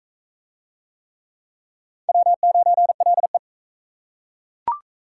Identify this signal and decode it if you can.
Morse code, heard as a keyed tone, about 40 wpm, 700 Hz. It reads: W9LE